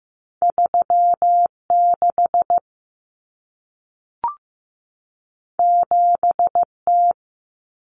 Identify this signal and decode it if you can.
Morse code, heard as a keyed tone, about 15 wpm, 700 Hz. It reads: 36 7T
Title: Common Cut Numbers - 2-Digit 15wpm